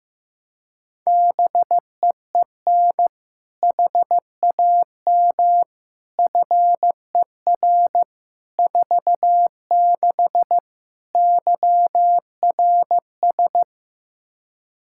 Morse code, heard as a keyed tone, about 15 wpm, 700 Hz. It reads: BEEN HAM FER 46 YRS
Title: K1USN SST QSO Practice 15wpm